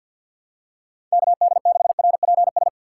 Morse code, heard as a keyed tone, about 52 wpm, 700 Hz. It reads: KD6RPS
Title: USA Call Signs - 6 Characters - 3x Character Spacing - 50wpm